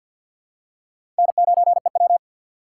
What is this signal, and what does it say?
Morse code, heard as a keyed tone, about 52 wpm, 700 Hz. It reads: N9EW
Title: USA Call Signs - 6x Character Spacing - 50wpm